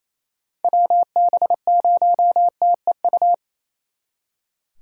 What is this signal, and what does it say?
Morse code, heard as a keyed tone, about 37 wpm, 700 Hz. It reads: WB0TEU